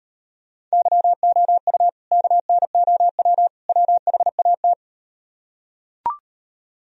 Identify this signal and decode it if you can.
Morse code, heard as a keyed tone, about 38 wpm, 700 Hz. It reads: YOU KNOW WHAT